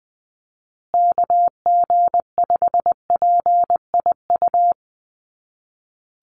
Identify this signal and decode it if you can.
Morse code, heard as a keyed tone, about 20 wpm, 700 Hz. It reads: KG5PIU
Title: USA Call Signs Repeated 3x 20wpm